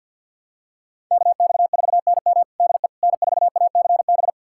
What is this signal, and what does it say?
Morse code, heard as a keyed tone, about 52 wpm, 700 Hz. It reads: KX4NM DE N4ACB
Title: USA Callsign DE Callsign 50wpm